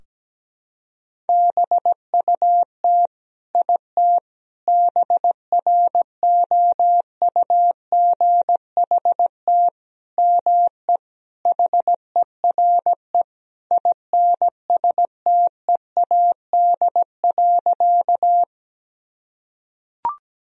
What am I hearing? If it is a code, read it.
Morse code, heard as a keyed tone, about 17 wpm, 700 Hz. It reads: BUT IT BROUGHT ME HERE INSTEAD.
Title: Sentences from Top 1000 Words - Review Part 4 - 17wpm